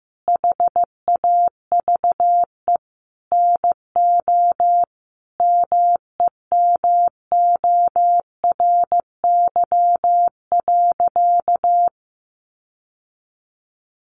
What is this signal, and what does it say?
Morse code, heard as a keyed tone, about 15 wpm, 700 Hz. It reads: HAVE NO MEMORY.
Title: Sentences from Top 1300 Words 15wpm